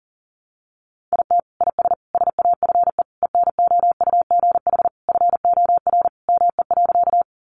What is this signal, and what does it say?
Morse code, heard as a keyed tone, about 40 wpm, 700 Hz. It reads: IT IS SAFE ENOUGH FOR ME.
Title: Sentences from Top 900 Words 40wpm